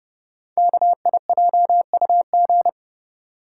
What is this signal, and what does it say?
Morse code, heard as a keyed tone, about 30 wpm, 700 Hz. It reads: KIJUG
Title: Code Groups - 5 Letters - 30wpm